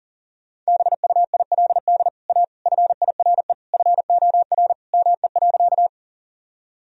Morse code, heard as a keyed tone, about 37 wpm, 700 Hz. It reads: BUILD A FIRE FOR ME.
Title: Sentences from Top 1000 Words - Review Part 1 - 40wpm